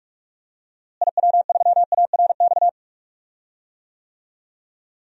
Morse code, heard as a keyed tone, about 42 wpm, 700 Hz. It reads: IW3ARX
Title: DXCC-100 Call Signs - 6 Characters - Repeated 3x - 45wpm